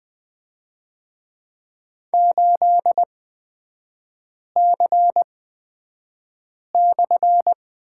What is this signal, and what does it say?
Morse code, heard as a keyed tone, about 20 wpm, spaced as for 5 wpm, 700 Hz. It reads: 8C/
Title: 255 Lesson 41e - Period - Three Characters 20wpm